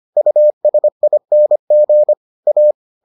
Morse code, heard as a keyed tone, about 25 wpm, 600 Hz. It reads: USING A